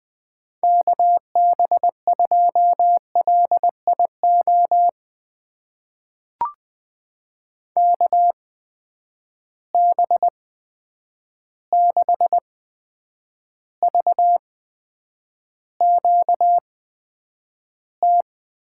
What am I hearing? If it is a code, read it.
Morse code, heard as a keyed tone, about 20 wpm, spaced as for 9 wpm, 700 Hz. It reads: KB2LIO KB6VQT
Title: USA Call Signs - 6 Characters - 8x Character Spacing - 20wpm